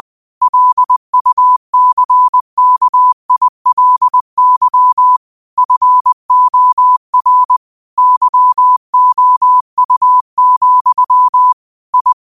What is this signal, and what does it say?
Morse code, heard as a keyed tone, about 20 wpm, 1000 Hz. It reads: LUCKILY FOR YOU, I
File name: qotdfiles / 20wpm